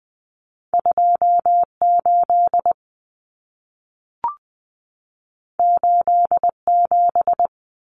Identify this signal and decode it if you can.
Morse code, heard as a keyed tone, about 20 wpm, 700 Hz. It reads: 28 87
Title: Common Cut Numbers - 2-Digit 20wpm